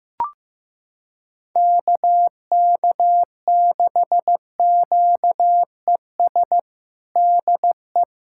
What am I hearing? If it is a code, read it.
Morse code, heard as a keyed tone, about 15 wpm, 700 Hz. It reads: KK6QES DE
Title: USA Callsign DE Callsign - 6 Characters - 15wpm